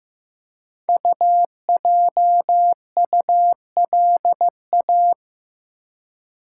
Morse code, heard as a keyed tone, about 15 wpm, 700 Hz. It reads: UJULA
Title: Code Groups - 5 Letters - 15wpm